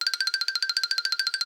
There is no code.